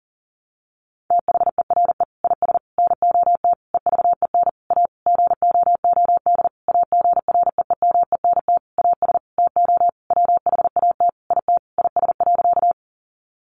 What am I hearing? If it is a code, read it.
Morse code, heard as a keyed tone, about 40 wpm, 700 Hz. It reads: THERE IS NOT EVEN A GOOD AGREEMENT AS TO WHAT IT IS.
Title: Sentences from Top 1500 Words 40wpm